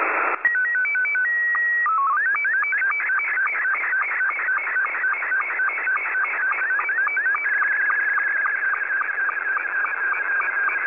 SSTV (slow-scan television), Scottie 2.
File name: SSTV images / websdr recording start 14230.0kHz